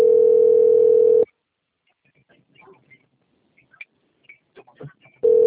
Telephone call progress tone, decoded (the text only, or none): ringback tone